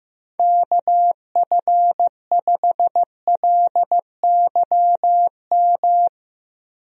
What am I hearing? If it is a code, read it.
Morse code, heard as a keyed tone, about 15 wpm, 700 Hz. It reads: KF5LYM
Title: USA Call Signs - 6 Characters - 15wpm